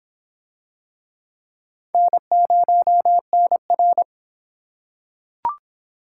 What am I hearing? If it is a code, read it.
Morse code, heard as a keyed tone, about 26 wpm, 700 Hz. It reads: N0NR